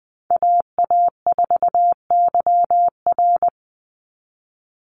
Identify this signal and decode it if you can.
Morse code, heard as a keyed tone, about 20 wpm, 700 Hz. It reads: AA4YR
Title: USA Call Signs 20wpm